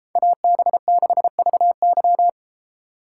Morse code, heard as a keyed tone, about 31 wpm, 700 Hz. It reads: AB6VY